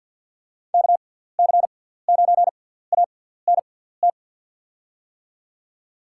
Morse code, heard as a keyed tone, about 53 wpm, spaced as for 17 wpm, 700 Hz. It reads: KC9ANT
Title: USA Call Signs - 6x Character Spacing - 50wpm